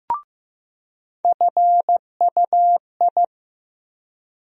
Morse code, heard as a keyed tone, about 15 wpm, 700 Hz. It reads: FUI
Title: USA Call Sign Postfixes - 3 Letters 15wpm